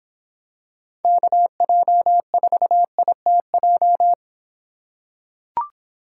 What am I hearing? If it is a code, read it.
Morse code, heard as a keyed tone, about 24 wpm, 700 Hz. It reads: KJ4ITJ